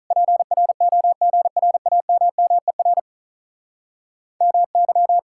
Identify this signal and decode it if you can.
Morse code, heard as a keyed tone, about 46 wpm, 700 Hz. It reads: PROGRAMMER MY